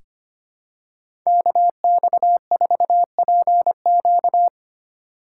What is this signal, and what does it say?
Morse code, heard as a keyed tone, about 25 wpm, 700 Hz. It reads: KX4PQ
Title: K1USN SST QSO Practice - 25wpm